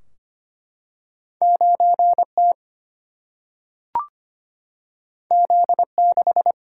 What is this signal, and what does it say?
Morse code, heard as a keyed tone, about 25 wpm, 700 Hz. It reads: 9T Z6
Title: Single-Letter Single-Number 25wpm